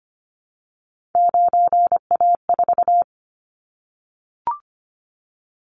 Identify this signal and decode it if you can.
Morse code, heard as a keyed tone, about 25 wpm, 700 Hz. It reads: 9A4
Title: Callsign Prefixes for Top 100 DXCC - 25wpm